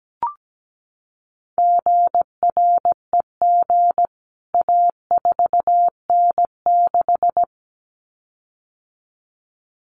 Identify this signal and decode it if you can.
Morse code, heard as a keyed tone, about 17 wpm, 700 Hz. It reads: GREG A4N6
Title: CWT Exchange 17wpm